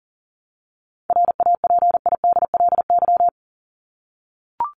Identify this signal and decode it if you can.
Morse code, heard as a keyed tone, about 40 wpm, 700 Hz. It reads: RAPIDLY
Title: LY Suffix - Top 100 Words with Base Words 40wpm